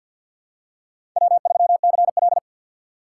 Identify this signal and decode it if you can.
Morse code, heard as a keyed tone, about 53 wpm, 700 Hz. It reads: W3CL
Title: USA Call Signs - 4 Characters - 6x Character Spacing - 50wpm